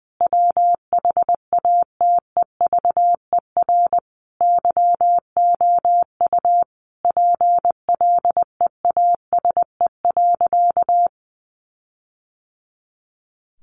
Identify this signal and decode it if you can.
Morse code, heard as a keyed tone, about 20 wpm, 700 Hz. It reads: WHATEVER YOU PLEASE.
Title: Sentences from Top 1500 Words 20wpm